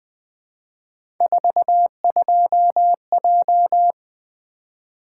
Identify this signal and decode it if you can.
Morse code, heard as a keyed tone, about 20 wpm, 700 Hz. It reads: 42J
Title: Three Letter-Number Combinations - 20wpm